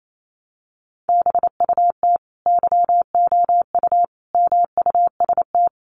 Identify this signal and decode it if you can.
Morse code, heard as a keyed tone, about 28 wpm, 700 Hz. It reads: BUT YOU MUST
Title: Sentences from Top 1300 Words 28wpm